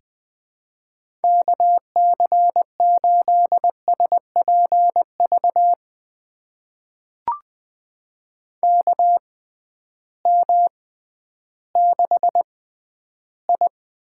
Morse code, heard as a keyed tone, about 20 wpm, spaced as for 12 wpm, 700 Hz. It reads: KC8SPV KM6I
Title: USA Call Signs - 6 Characters - 6x Character Spacing - 20wpm